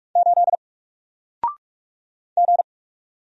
Morse code, heard as a keyed tone, about 44 wpm, 700 Hz. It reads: 8 G
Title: Single Character (Letter:Number:Punctuation:Prosign) 45wpm